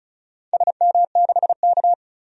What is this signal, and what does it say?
Morse code, heard as a keyed tone, about 36 wpm, 700 Hz. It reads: SM6K
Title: DXCC-100 Call Signs - 4 Characters - 35wpm